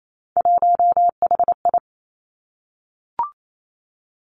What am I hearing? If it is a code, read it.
Morse code, heard as a keyed tone, about 28 wpm, 700 Hz. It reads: 1HI
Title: Three Challenging Letter-Number Combinations 28wpm